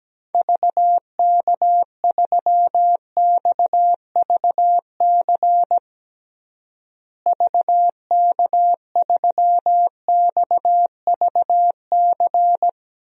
Morse code, heard as a keyed tone, about 17 wpm, 700 Hz. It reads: VK3XVC VK3XVC
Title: DXCC-100 Call Signs - 6 Characters - Repeated 3x - 17wpm